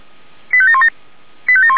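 Five-tone selective call (ZVEI), heard as quick runs of five tones, tones of about 70 ms. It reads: 85717
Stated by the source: ZVEI five-tone call 85371 received by SDR, channel 464, recorded by Feuerwehr Orlamünde.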